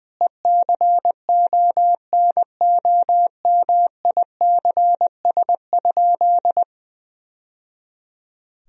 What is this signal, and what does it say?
Morse code, heard as a keyed tone, about 20 wpm, 700 Hz. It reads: ECONOMICS?
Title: Sentences from Top 1000 Words - Review Part 3 - 20wpm